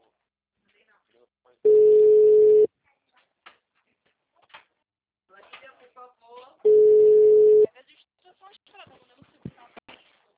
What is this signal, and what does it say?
Ringback tone (Europe, 425 Hz), with speech.